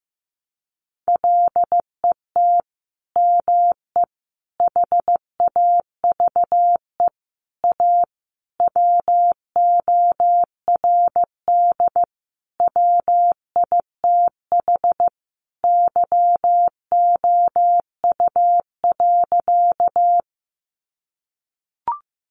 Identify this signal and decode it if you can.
Morse code, heard as a keyed tone, about 15 wpm, 700 Hz. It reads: LET ME HAVE A WORD WITH YOU.
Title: Sentences from Top 300 Words 15wpm